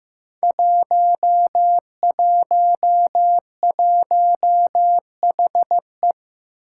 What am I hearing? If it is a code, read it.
Morse code, heard as a keyed tone, about 15 wpm, 700 Hz. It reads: 111HE
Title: Five Challenging Letter-Number Combinations 15wpm